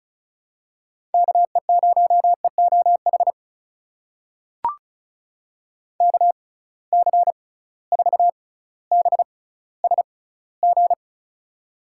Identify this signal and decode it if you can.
Morse code, heard as a keyed tone, about 34 wpm, spaced as for 18 wpm, 700 Hz. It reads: KE0EOH KC4BSG
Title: USA Call Signs - 6 Characters - 6x Character Spacing - 35wpm